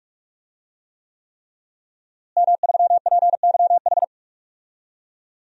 Morse code, heard as a keyed tone, about 47 wpm, 700 Hz. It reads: M3PYH
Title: DXCC-100 Call Signs - 5 Characters - Repeated 3x - 45wpm